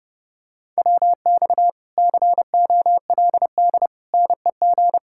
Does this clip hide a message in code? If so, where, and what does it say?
Yes: Morse code from 0.8 to 5.0 s, heard as a keyed tone, about 30 wpm, 700 Hz. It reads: WX COLD NEG